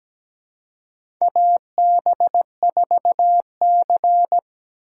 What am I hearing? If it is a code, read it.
Morse code, heard as a keyed tone, about 17 wpm, 700 Hz. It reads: AB4C
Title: USA Call Signs - 3x Character Spacing - 17wpm